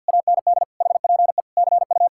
Morse code, heard as a keyed tone, about 66 wpm, spaced as for 57 wpm, 700 Hz. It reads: AND HPE CU
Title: Speed-Racing Common QSO Phrases - 50wpm